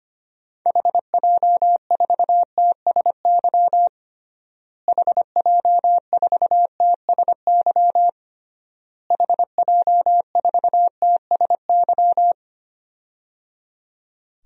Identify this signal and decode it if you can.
Morse code, heard as a keyed tone, about 25 wpm, 700 Hz. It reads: HJ4TSY HJ4TSY HJ4TSY
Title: DXCC-100 Call Signs - 6 Characters - Repeated 3x - 25wpm